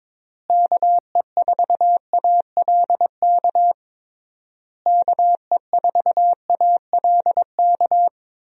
Morse code, heard as a keyed tone, about 22 wpm, 700 Hz. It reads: KE4ALK KE4ALK
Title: USA Call Signs - 6 Characters - Repeated 3x - 22wpm